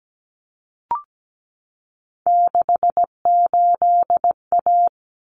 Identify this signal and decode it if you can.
Morse code, heard as a keyed tone, about 17 wpm, 700 Hz. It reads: 68A
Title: Common Cut Numbers - 3-Digit 17wpm